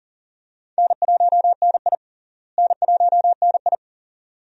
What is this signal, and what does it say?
Morse code, heard as a keyed tone, about 40 wpm, 700 Hz. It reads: N1NI N1NI
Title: USA Call Signs - 4 Characters - Repeated 2x - 40wpm